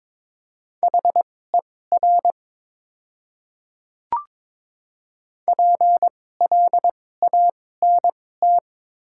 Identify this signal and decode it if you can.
Morse code, heard as a keyed tone, about 22 wpm, spaced as for 16 wpm, 700 Hz. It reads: HER PLANT
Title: 2x Character Spacing - Top 500 Words - 22wpm